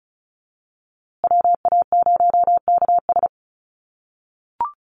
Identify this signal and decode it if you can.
Morse code, heard as a keyed tone, about 34 wpm, 700 Hz. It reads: WA0KS
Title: USA Call Signs Repeated 3x 35wpm